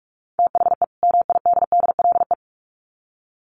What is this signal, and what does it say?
Morse code, heard as a keyed tone, about 44 wpm, 700 Hz. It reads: THE MIDDLE
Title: Sets of 2 Words 45wpm